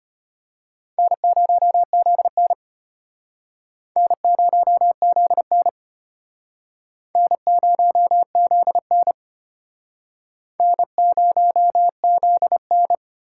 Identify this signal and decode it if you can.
Morse code, heard as a keyed tone, about 35 wpm, 700 Hz. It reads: N0ZN N0ZN N0ZN N0ZN